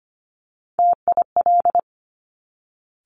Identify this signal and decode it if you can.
Morse code, heard as a keyed tone, about 25 wpm, 700 Hz. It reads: TIL
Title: Top Three Letter Combinations in Words 25wpm